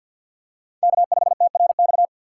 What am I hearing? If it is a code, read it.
Morse code, heard as a keyed tone, about 53 wpm, 700 Hz. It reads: K5TRX
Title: USA Call Signs - 5 Characters - 6x Character Spacing - 50wpm